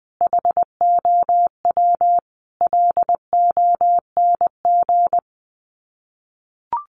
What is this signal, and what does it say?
Morse code, heard as a keyed tone, about 20 wpm, 700 Hz. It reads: HOW LONG
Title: Sets of 5 Words Encore2 20wpm